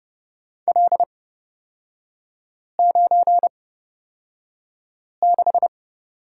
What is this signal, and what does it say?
Morse code, heard as a keyed tone, about 31 wpm, spaced as for 5 wpm, 700 Hz. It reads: L96
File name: Lesson 36e - BT - Three Characters 30wpm